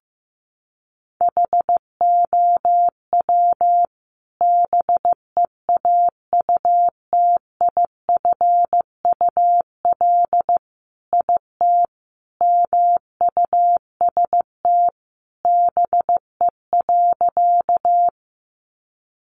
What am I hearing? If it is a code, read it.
Morse code, heard as a keyed tone, about 15 wpm, 700 Hz. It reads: HOW BEAUTIFUL IT MUST BE.
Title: Sentences from Top 900 Words 15wpm